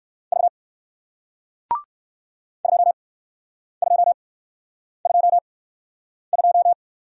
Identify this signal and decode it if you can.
Morse code, heard as a keyed tone, about 55 wpm, spaced as for 9 wpm, 700 Hz. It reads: V 2222
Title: Speed-Racing Single Letter-Number 45wpm